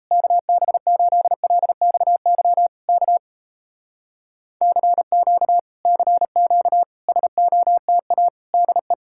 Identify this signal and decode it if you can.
Morse code, heard as a keyed tone, about 38 wpm, 700 Hz. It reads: KB8LXY K CQ CQ SOTA DE